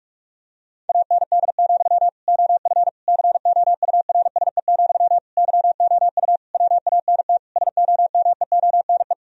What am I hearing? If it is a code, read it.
Morse code, heard as a keyed tone, about 46 wpm, 700 Hz. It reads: AND, OF COURSE, YOU WANT SOMEONE
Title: Sentences from Top 1000 Words - Review Part 1 - 45wpm